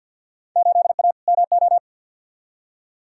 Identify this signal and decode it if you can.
Morse code, heard as a keyed tone, about 48 wpm, 700 Hz. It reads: 8A MO
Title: Field Day Exchanges Encore - 50wpm